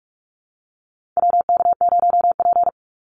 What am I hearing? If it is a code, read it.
Morse code, heard as a keyed tone, about 44 wpm, 700 Hz. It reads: WK0P
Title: Call Signs - 45wpm